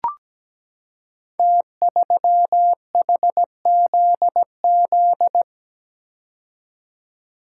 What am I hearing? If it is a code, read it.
Morse code, heard as a keyed tone, about 17 wpm, 700 Hz. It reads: T3HZZ
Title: Code Groups - 5 Letter:Numbers - 17wpm